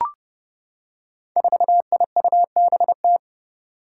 Morse code, heard as a keyed tone, about 30 wpm, 700 Hz. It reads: 4IUBT
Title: Five Challenging Letter-Number Combinations 30wpm